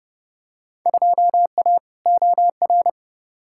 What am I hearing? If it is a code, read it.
Morse code, heard as a keyed tone, about 32 wpm, 700 Hz. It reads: 2A OR
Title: Field Day Exchanges - 30wpm